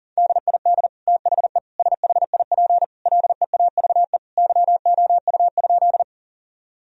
Morse code, heard as a keyed tone, about 40 wpm, 700 Hz. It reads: DID THE SHIP LEAVE YOU?
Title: Sentences from Top 1000 Words - Review Part 2 - 40wpm